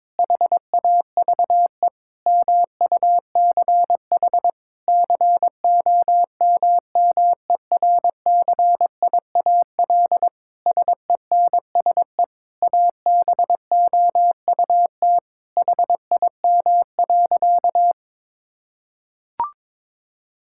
Morse code, heard as a keyed tone, about 22 wpm, 700 Hz. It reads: HAVE MUCH COMMERCIAL SENSE ABOUT HIM.
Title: Sentences from Top 1000 Words - Review Part 5 - 22wpm